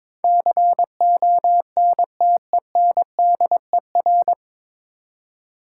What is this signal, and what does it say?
Morse code, heard as a keyed tone, about 22 wpm, 700 Hz. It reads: CONTENDER
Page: Morse Code Ninja practice 22wpm